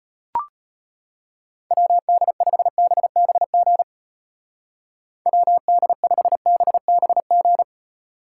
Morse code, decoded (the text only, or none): WD5BBG WD5BBG